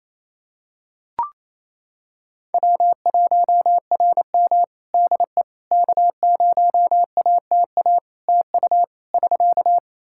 Morse code, heard as a keyed tone, about 28 wpm, 700 Hz. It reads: W1RM DE K0ATA TU <SK>
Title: Practice QSO - Exchange 4 - Ending - 28wpm